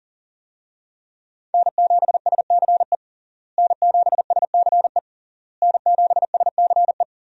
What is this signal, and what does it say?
Morse code, heard as a keyed tone, about 40 wpm, 700 Hz. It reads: N7SCE N7SCE N7SCE
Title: USA Call Signs - 5 Characters - Repeated 3x - 40wpm